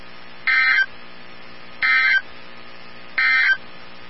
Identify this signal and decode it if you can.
Emergency Alert System (SAME) tones, heard as harsh buzzing bursts. It is heard from 0.3 to 3.6 s.
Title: NOAA Weather Radio 162.550